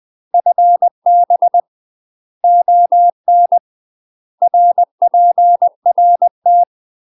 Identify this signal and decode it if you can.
Morse code, heard as a keyed tone, about 20 wpm, 700 Hz. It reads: FB ON RPRT
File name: UK MASTER QSOs TRAINING/20WPM